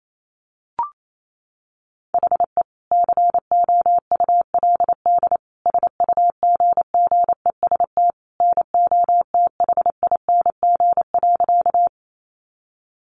Morse code, heard as a keyed tone, about 28 wpm, 700 Hz. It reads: HE COULD SUGGEST NOTHING.
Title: Sentences from Top 1300 Words 28wpm